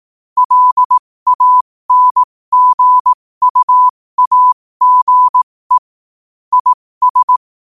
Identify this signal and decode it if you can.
Morse code, heard as a keyed tone, about 18 wpm, spaced as for 15 wpm, 1000 Hz. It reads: LANGUAGE IS
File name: qotdfiles / 15wpm